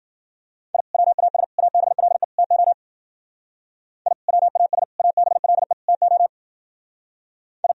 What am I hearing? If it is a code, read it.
Morse code, heard as a keyed tone, about 51 wpm, 700 Hz. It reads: I WAS ABLE TO I WAS ABLE TO I